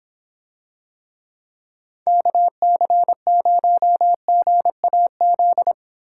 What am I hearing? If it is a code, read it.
Morse code, heard as a keyed tone, about 26 wpm, 700 Hz. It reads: KC0GAZ